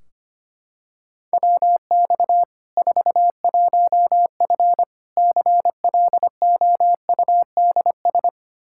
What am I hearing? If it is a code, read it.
Morse code, heard as a keyed tone, about 25 wpm, 700 Hz. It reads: WX 41F CLOUDS
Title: Practice QSO - Exchange 2 - Info - 25wpm